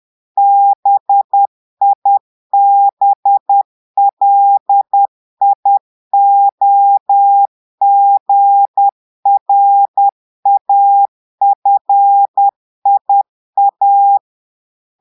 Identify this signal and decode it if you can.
Morse code, heard as a keyed tone, about 10 wpm, 800 Hz. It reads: BIBLIOGRAFIA
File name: Portugais - texte en clair 1 - vitesse - 10 mots minute